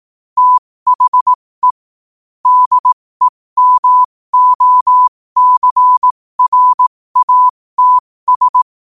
Morse code, heard as a keyed tone, about 18 wpm, spaced as for 15 wpm, 1000 Hz. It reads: THE DEMOCRATS
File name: qotdfiles / 15wpm